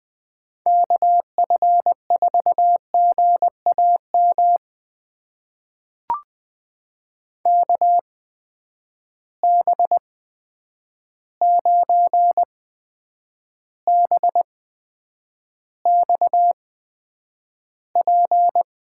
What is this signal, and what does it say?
Morse code, heard as a keyed tone, about 20 wpm, spaced as for 9 wpm, 700 Hz. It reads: KF4GAM KB9BXP
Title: USA Call Signs - 6 Characters - 8x Character Spacing - 20wpm